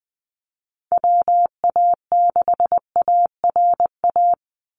Morse code, heard as a keyed tone, about 20 wpm, 700 Hz. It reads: WA6ARA
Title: NA SOTA Chaser Callsigns 20wpm